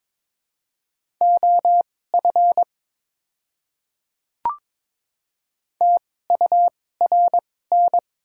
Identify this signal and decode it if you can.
Morse code, heard as a keyed tone, about 22 wpm, spaced as for 16 wpm, 700 Hz. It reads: OF TURN
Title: 2x Character Spacing - Top 300 Words - 22wpm